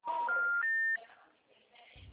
Special information tone.
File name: out- -2014